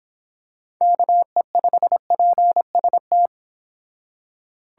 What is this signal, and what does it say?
Morse code, heard as a keyed tone, about 25 wpm, 700 Hz. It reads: KE5PST